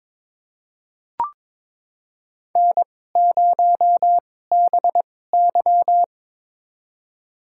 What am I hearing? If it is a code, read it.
Morse code, heard as a keyed tone, about 22 wpm, spaced as for 16 wpm, 700 Hz. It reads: N0BY
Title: USA Call Signs - 4 Characters - 2x Character Spacing - 22wpm